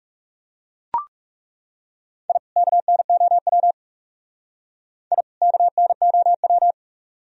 Morse code, heard as a keyed tone, about 36 wpm, 700 Hz. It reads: I KNOW I KNOW
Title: Speed-Racing - Sets of 2 Words 30wpm